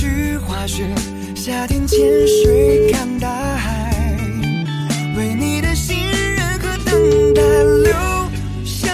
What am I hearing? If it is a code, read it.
Ringback tone (Europe, 425 Hz), with noise.